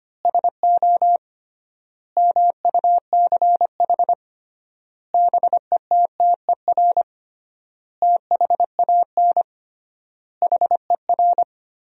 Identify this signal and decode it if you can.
Morse code, heard as a keyed tone, about 25 wpm, 700 Hz. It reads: SO MUCH BETTER THAN HER
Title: 3x Word-Spacing - Sentences from Top 400 Words - 25wpm